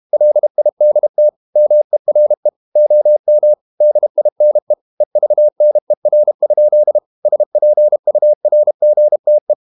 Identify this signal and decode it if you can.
Morse code, heard as a keyed tone, about 32 wpm, 600 Hz. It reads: LIDT MERE OM DINE EVNER? SPURGTE
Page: Ferie 32 wpm